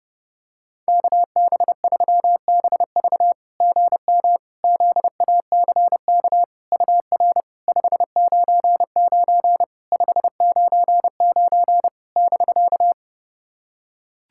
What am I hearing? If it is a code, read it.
Morse code, heard as a keyed tone, about 30 wpm, 700 Hz. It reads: KB3BV GM ZACK UR 599 599 <BK>
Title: SOTA QSO Practice 30wpm